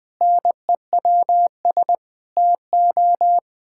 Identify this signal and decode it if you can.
Morse code, heard as a keyed tone, about 20 wpm, 700 Hz. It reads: NEWS TO
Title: Pangram Challenge v2 - 20wpm